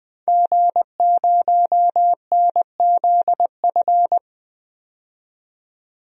Morse code, heard as a keyed tone, about 20 wpm, 700 Hz. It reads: G0NZF
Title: DXCC-100 Call Signs - 5 Characters - 20wpm